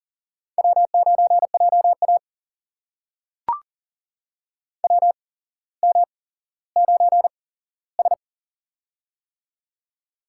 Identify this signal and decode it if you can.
Morse code, heard as a keyed tone, about 40 wpm, spaced as for 17 wpm, 700 Hz. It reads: W9JA WM9S
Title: USA Call Signs - 4 Characters - 8x Character Spacing - 40wpm